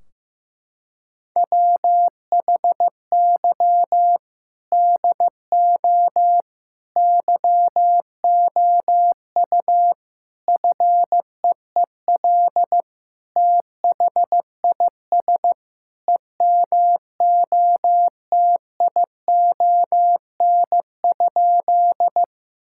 Morse code, heard as a keyed tone, about 15 wpm, 700 Hz. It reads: WHY DO YOU FEEL THIS EMOTION?
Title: Sentences from Top 1600 Words 15wpm